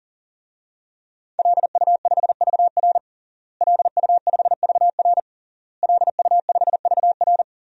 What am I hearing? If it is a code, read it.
Morse code, heard as a keyed tone, about 40 wpm, 700 Hz. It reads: LU5VR LU5VR LU5VR
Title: DXCC-100 Call Signs - 5 Characters - Repeated 3x - 40wpm